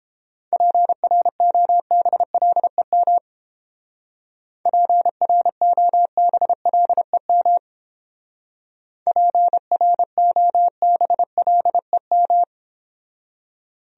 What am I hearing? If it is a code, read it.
Morse code, heard as a keyed tone, about 30 wpm, 700 Hz. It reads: PROBLEM PROBLEM PROBLEM